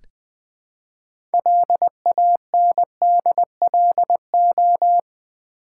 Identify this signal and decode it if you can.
Morse code, heard as a keyed tone, about 20 wpm, 700 Hz. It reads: LANDLO
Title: Build A Word - 3 Letters At A Time - 8-Letter Words - 20wpm